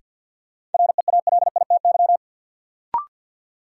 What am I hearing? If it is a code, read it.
Morse code, heard as a keyed tone, about 52 wpm, 700 Hz. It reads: REALITY
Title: Sentences from Top 1200 Words - New Word Drill - 50wpm